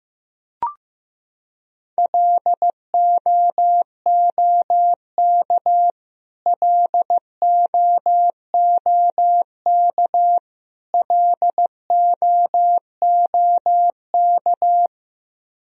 Morse code, heard as a keyed tone, about 15 wpm, 700 Hz. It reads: LOOK LOOK LOOK
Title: Top 100 Words Repeated 15wpm